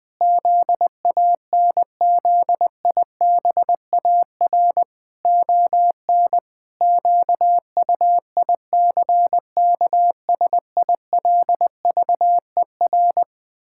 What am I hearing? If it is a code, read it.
Morse code, heard as a keyed tone, about 20 wpm, 700 Hz. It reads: ZANZIBAR ON QUICKSILVER
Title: Pangram Challenge v2 - 20wpm